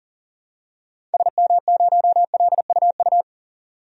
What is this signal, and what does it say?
Morse code, heard as a keyed tone, about 40 wpm, 700 Hz. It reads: SM0LUU
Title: DXCC-100 Call Signs - 6 Characters - 40wpm